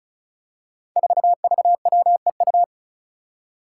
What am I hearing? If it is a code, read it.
Morse code, heard as a keyed tone, about 36 wpm, 700 Hz. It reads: 4VWEU
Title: Five Challenging Letter-Number Combinations 35wpm